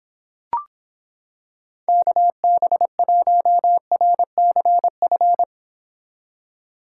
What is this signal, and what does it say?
Morse code, heard as a keyed tone, about 26 wpm, 700 Hz. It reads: KB1RCF